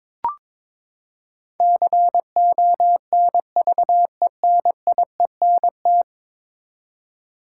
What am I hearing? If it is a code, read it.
Morse code, heard as a keyed tone, about 21 wpm, 700 Hz. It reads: CONVENIENT
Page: morse-code-ninja-practice-22wpm